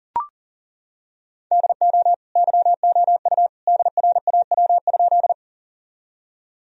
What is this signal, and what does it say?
Morse code, heard as a keyed tone, about 40 wpm, 700 Hz. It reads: DO YOU DRAW?
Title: Sentences from Top 1000 Words - Review Part 1 - 40wpm